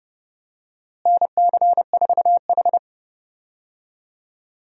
Morse code, heard as a keyed tone, about 30 wpm, 700 Hz. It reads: NC4H
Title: USA Call Signs - 4 Characters - Repeated 3x - 30wpm